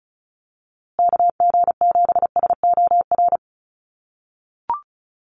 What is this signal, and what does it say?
Morse code, heard as a keyed tone, about 34 wpm, 700 Hz. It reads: KG7SOR
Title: USA Call Signs Repeated 2x 35wpm